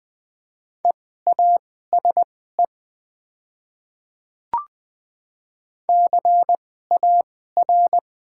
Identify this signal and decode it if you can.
Morse code, heard as a keyed tone, about 20 wpm, spaced as for 14 wpm, 700 Hz. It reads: EASE CAR
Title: 2x Character Spacing - Top 300 Words - 20wpm